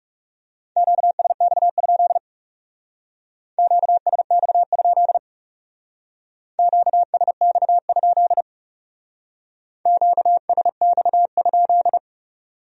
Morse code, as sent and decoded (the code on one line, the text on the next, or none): --.- ... -..- ..--.. / --.- ... -..- ..--.. / --.- ... -..- ..--.. / --.- ... -..- ..--..
QSX? QSX? QSX? QSX?